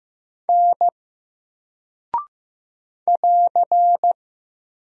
Morse code, heard as a keyed tone, about 15 wpm, 700 Hz. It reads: N <AR>
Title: Single Character (Letter:Number:Punctuation:Prosign) Alt 15wpm